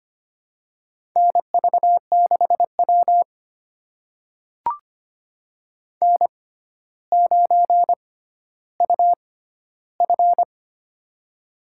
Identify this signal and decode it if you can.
Morse code, heard as a keyed tone, about 25 wpm, spaced as for 13 wpm, 700 Hz. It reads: NV6W N9UF
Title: USA Call Signs - 4 Characters - 6x Character Spacing - 25wpm